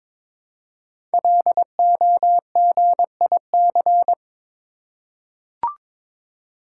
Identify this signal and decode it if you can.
Morse code, heard as a keyed tone, about 22 wpm, 700 Hz. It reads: LOGIC
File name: 5 Letter Words - First 3 Letters Given - 22wpm